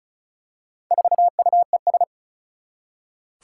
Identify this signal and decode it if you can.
Morse code, heard as a keyed tone, about 36 wpm, 700 Hz. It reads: 4UES
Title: Four Challenging Letter-Number Combinations 35wpm